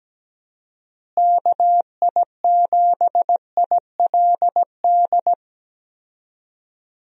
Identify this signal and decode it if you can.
Morse code, heard as a keyed tone, about 17 wpm, 700 Hz. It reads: KI7ILD
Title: USA Call Signs - 6 Characters - Repeated 3x - 17wpm